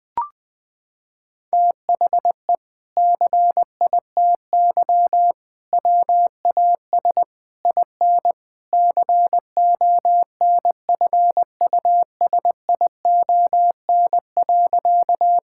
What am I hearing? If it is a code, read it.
Morse code, heard as a keyed tone, about 21 wpm, 700 Hz. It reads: THE CITY WAS IN CONFUSION.
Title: Sentences from Top 1600 Words 20wpm